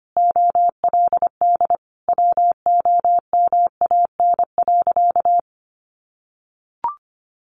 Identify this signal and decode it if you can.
Morse code, heard as a keyed tone, about 25 wpm, 700 Hz. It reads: OLD WOMAN.
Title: Sentences from Top 1100 Words 25wpm